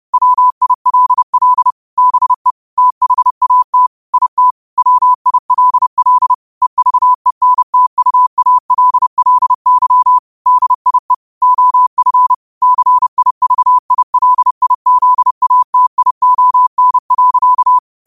Morse code, heard as a keyed tone, about 30 wpm, 1000 Hz. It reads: WILL BE THAT IT WILL EVENTUALLY DIE OF CIVILIZATION.
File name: qotdfiles / 30wpm